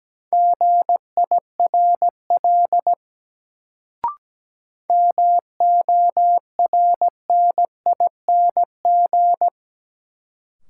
Morse code, heard as a keyed tone, about 17 wpm, 700 Hz. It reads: GIRL MORNING
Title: Sentences from Top 400 Words - New Word Drill - 17wpm